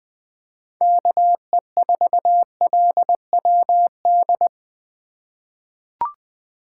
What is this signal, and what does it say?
Morse code, heard as a keyed tone, about 21 wpm, 700 Hz. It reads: KE4LWD
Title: USA Call Signs - 6 Characters - 3x Character Spacing - 20wpm